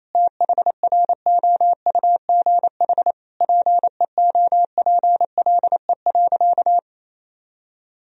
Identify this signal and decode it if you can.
Morse code, heard as a keyed tone, about 28 wpm, 700 Hz. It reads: THROUGH PEOPLE.
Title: Sentences from Top 200 Words and Abbreviations 28wpm